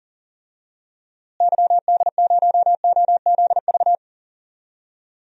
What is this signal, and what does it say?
Morse code, heard as a keyed tone, about 40 wpm, 700 Hz. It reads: YD0OZV
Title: DXCC-100 Call Signs - 6 Characters - Repeated 2x - 40wpm